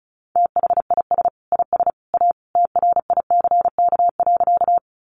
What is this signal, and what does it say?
Morse code, heard as a keyed tone, about 34 wpm, 700 Hz. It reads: THIS IS A TRICK.
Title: Sentences from Top 1500 Words 35wpm